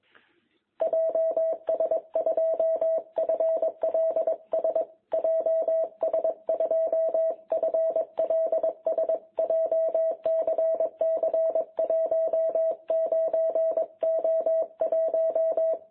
Morse code, heard as a keyed tone, about 21 wpm, 660 Hz. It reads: JS2FLS JS2FLS JCC 19O1
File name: jh8cbh / 2021cw